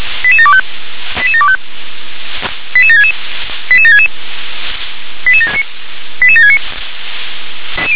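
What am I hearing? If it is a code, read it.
Five-tone selective call (ZVEI), heard as quick runs of five tones, tones of about 70 ms. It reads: 88725 88725 88766 88766 88769
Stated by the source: ZVEI five-tone selective call, received by SDR.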